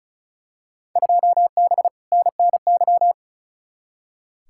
Morse code, heard as a keyed tone, about 36 wpm, 700 Hz. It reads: 2B NNY
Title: Field Day Exchanges Encore - 35wpm